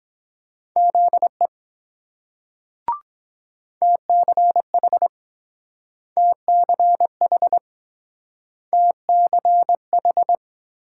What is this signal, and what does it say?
Morse code, heard as a keyed tone, about 23 wpm, 700 Hz. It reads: ZE TCH TCH TCH